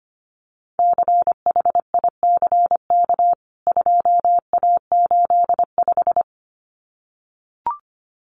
Morse code, heard as a keyed tone, about 25 wpm, 700 Hz. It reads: CHICK 2A85
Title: CWT Exchange 25wpm